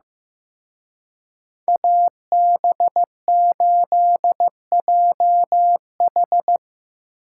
Morse code, heard as a keyed tone, about 17 wpm, 700 Hz. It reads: AB8JH
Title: Speed Racing - USA 2x2 Call Signs 15wpm